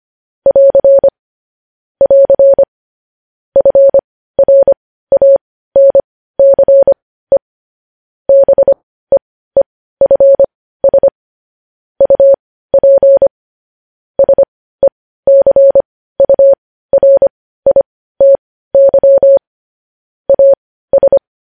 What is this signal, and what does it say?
Morse code, heard as a keyed tone, about 25 wpm, spaced as for 15 wpm, 550 Hz. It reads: <AR> <AR> FRANCE BEEFS UP SECURITY AS